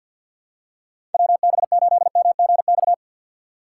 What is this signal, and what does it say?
Morse code, heard as a keyed tone, about 48 wpm, 700 Hz. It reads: WB8MGX
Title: USA Call Signs - 6 Characters - 4x Character Spacing - 50wpm